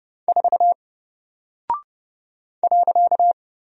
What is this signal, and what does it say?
Morse code, heard as a keyed tone, about 30 wpm, 700 Hz. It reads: 4 .
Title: Single Character (Letter:Number:Punctuation:Prosign) Alt 30wpm